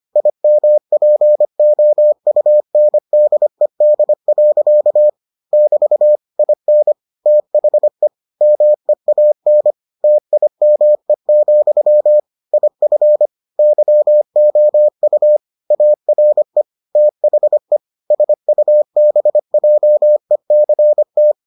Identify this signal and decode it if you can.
Morse code, heard as a keyed tone, about 25 wpm, 600 Hz. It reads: IMPOUNDED. <BT> IN THE MEAN TIME, IF YOU ARE THE SUBJECT